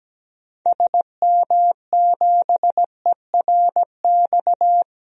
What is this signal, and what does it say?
Morse code, heard as a keyed tone, about 17 wpm, 700 Hz. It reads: SM7ERX
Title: DXCC-100 Call Signs - 6 Characters - Repeated 3x - 17wpm